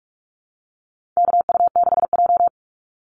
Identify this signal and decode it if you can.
Morse code, heard as a keyed tone, about 44 wpm, 700 Hz. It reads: KU6J
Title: NA SOTA Activator Callsigns 45wpm